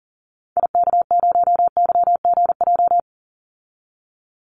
Morse code, heard as a keyed tone, about 40 wpm, 700 Hz. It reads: IK0YGJ
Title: Call Signs Repeated 2x - 40wpm